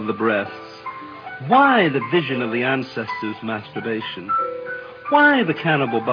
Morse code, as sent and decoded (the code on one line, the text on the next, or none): none